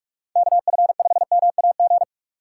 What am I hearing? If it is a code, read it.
Morse code, heard as a keyed tone, about 44 wpm, 700 Hz. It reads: KF5MAG